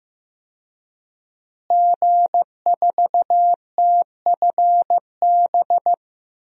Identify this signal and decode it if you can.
Morse code, heard as a keyed tone, about 15 wpm, 700 Hz. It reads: G4TFB
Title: DXCC-100 Call Signs - 5 Characters - Repeated 3x - 15wpm